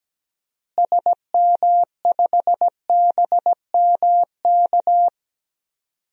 Morse code, heard as a keyed tone, about 17 wpm, 700 Hz. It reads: SM5BMK
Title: DXCC-100 Call Signs - 6 Characters - Repeated 2x - 17wpm